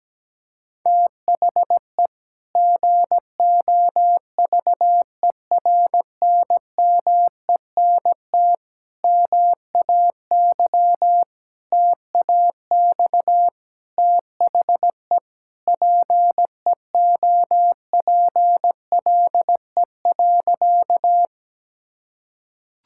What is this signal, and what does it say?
Morse code, heard as a keyed tone, about 17 wpm, 700 Hz. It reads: THE GOVERNMENT MAY TAX THE PEOPLE.
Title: Sentences from Top 1000 Words - Review Part 5 - 17wpm